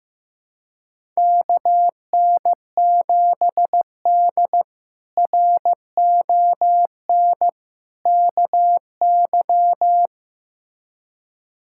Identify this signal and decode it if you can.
Morse code, heard as a keyed tone, about 15 wpm, 700 Hz. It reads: KN7D RON KY
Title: K1USN SST QSO Practice - 15wpm